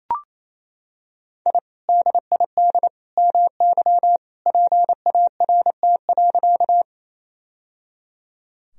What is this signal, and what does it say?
Morse code, heard as a keyed tone, about 28 wpm, 700 Hz. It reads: I DID MY PART.
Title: Sentences from Top 1000 Words - Review Part 1 - 28wpm